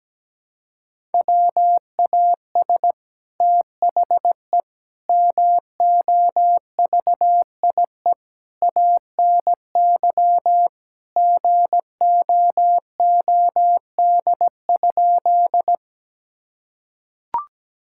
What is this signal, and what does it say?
Morse code, heard as a keyed tone, about 17 wpm, 700 Hz. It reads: WAS THE MOVIE ANY GOOD?
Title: Sentences from Top 1000 Words - Review Part 5 - 17wpm